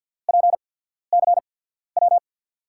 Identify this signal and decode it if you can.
Morse code, heard as a keyed tone, about 52 wpm, spaced as for 14 wpm, 700 Hz. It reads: PCW
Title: USA Call Signs - 5 Characters - 8x Character Spacing - 50wpm